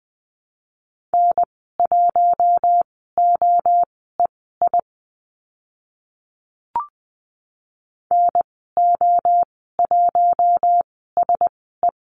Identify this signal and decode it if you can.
Morse code, heard as a keyed tone, about 20 wpm, spaced as for 14 wpm, 700 Hz. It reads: N1OEI NO1SE
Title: 042 Lesson 8g - Number 1 - Call Signs 20wpm